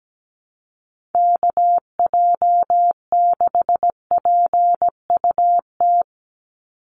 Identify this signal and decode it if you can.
Morse code, heard as a keyed tone, about 17 wpm, 700 Hz. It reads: KJ6PUT
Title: USA Call Signs 17wpm